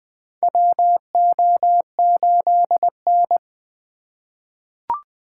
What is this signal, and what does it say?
Morse code, heard as a keyed tone, about 20 wpm, 700 Hz. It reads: WO8N
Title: USA Call Signs - 4 Characters - 8x Character Spacing - 20wpm